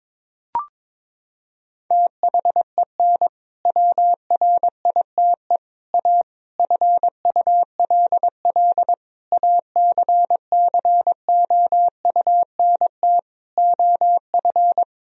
Morse code, heard as a keyed tone, about 22 wpm, 700 Hz. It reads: THEN WRITE A FULL ACCOUNT OF
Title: Sentences from Top 1000 Words - Review Part 3 - 22wpm